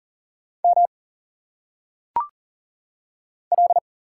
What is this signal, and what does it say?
Morse code, heard as a keyed tone, about 40 wpm, 700 Hz. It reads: M L
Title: Single Letters - Phonetic Pronunciation 40wpm